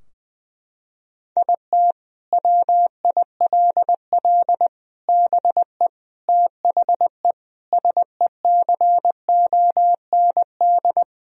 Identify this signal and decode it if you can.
Morse code, heard as a keyed tone, about 20 wpm, 700 Hz. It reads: IT WILL BE THE SECOND
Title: Sentences from Top 1000 Words - Review Part 2 - 20wpm